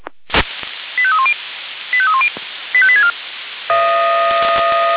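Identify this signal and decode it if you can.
Five-tone selective call (ZVEI), heard as quick runs of five tones, tones of about 70 ms. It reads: 85310 85310 85864
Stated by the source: ZVEI five-tone selective call received on SDR.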